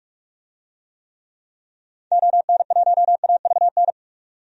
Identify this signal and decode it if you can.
Morse code, heard as a keyed tone, about 42 wpm, 700 Hz. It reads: ON1AVN
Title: DXCC-100 Call Signs - 6 Characters - 45wpm